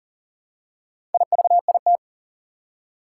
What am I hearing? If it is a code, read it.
Morse code, heard as a keyed tone, about 44 wpm, 690 Hz. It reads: IVIT
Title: Four Challenging Letter-Number Combinations 40wpm